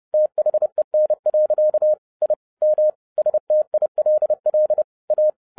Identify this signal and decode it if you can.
Morse code, heard as a keyed tone, about 30 wpm, 610 Hz. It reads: THEN. I M STILL A